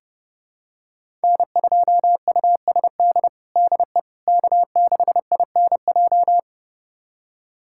Morse code, heard as a keyed tone, about 30 wpm, 700 Hz. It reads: N2USD DE K6INJ
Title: USA Callsign DE Callsign - 5 Characters - 30wpm